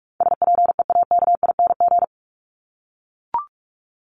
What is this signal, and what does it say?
Morse code, heard as a keyed tone, about 44 wpm, 700 Hz. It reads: SPEAKING
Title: ING Suffix - Top 100 Words with Base Words 45wpm